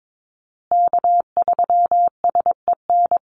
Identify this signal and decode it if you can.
Morse code, heard as a keyed tone, about 22 wpm, 700 Hz. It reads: K3SEN
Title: Call Signs Repeated 2x - 22wpm